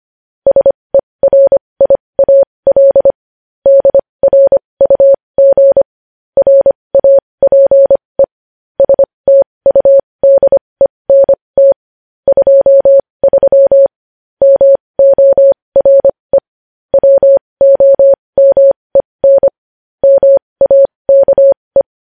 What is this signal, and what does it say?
Morse code, heard as a keyed tone, about 25 wpm, spaced as for 20 wpm, 550 Hz. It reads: SERIAL DRUG RAPE STUDENT 23 MORE WOMEN MAKE
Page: news-headlines-in-morse-code-at-20-wpm